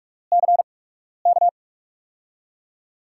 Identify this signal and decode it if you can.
Morse code, heard as a keyed tone, about 46 wpm, 700 Hz. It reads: C K
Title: USA Call Signs - 6 Characters - 8x Character Spacing - 45wpm